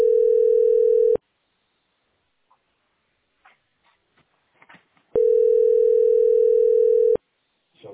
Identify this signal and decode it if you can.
Ringback tone (North America), 440 + 480 Hz.